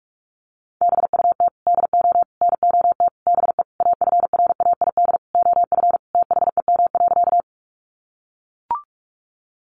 Morse code, heard as a keyed tone, about 44 wpm, 700 Hz. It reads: BUT DO NOT BE AFRAID OF THEM.
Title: Sentences from Top 1500 Words 45wpm